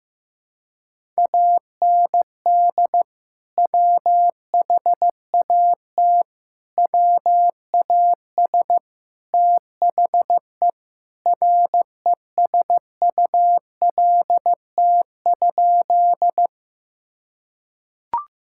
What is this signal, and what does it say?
Morse code, heard as a keyed tone, about 15 wpm, 700 Hz. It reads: AND WHAT WAS THE RESULT?
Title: Sentences from Top 1000 Words - Review Part 5 - 15wpm